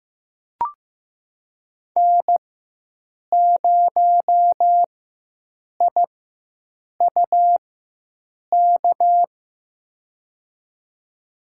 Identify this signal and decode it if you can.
Morse code, heard as a keyed tone, about 15 wpm, spaced as for 7 wpm, 700 Hz. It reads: N0IUK
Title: USA Call Signs - 4x Character Spacing - 15wpm